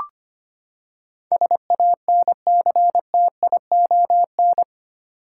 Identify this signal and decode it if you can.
Morse code, heard as a keyed tone, about 25 wpm, 700 Hz. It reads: SANCTION
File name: Eight Letter Words - 25wpm